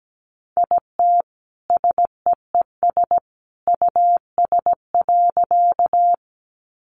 Morse code, heard as a keyed tone, about 17 wpm, 700 Hz. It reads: IT SEES US.
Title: A Sound Of Thunder 17wpm